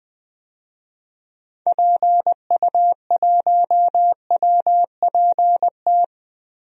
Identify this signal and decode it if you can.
Morse code, heard as a keyed tone, about 20 wpm, 700 Hz. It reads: PU1WPT
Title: DXCC-100 Call Signs - 6 Characters - 20wpm